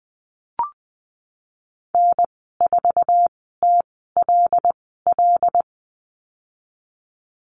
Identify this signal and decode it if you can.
Morse code, heard as a keyed tone, about 20 wpm, spaced as for 15 wpm, 700 Hz. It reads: N4TLL
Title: 122 Lesson 20g - Number 3 - Call Signs 20wpm